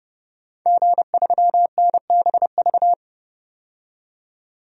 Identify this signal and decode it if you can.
Morse code, heard as a keyed tone, about 31 wpm, 700 Hz. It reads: G3NBV
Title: DXCC-100 Call Signs - 5 Characters - 30wpm